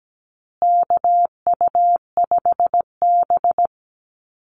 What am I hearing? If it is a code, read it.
Morse code, heard as a keyed tone, about 17 wpm, 700 Hz. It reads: KU5B
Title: Call Signs Repeated 2x - 17wpm